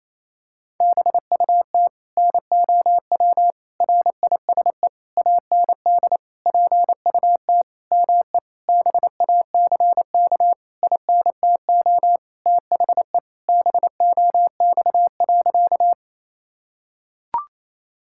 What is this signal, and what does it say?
Morse code, heard as a keyed tone, about 27 wpm, 700 Hz. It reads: BUT NOW RISE AND PUT ME BACK INTO THE BOX.
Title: Sentences from Top 1000 Words - Review Part 6 - 28wpm